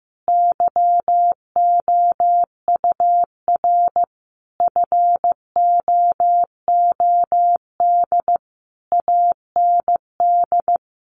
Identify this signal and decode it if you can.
Morse code, heard as a keyed tone, about 15 wpm, 700 Hz. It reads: YOUR FOOD AND
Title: Sentences from Top 800 Words 15wpm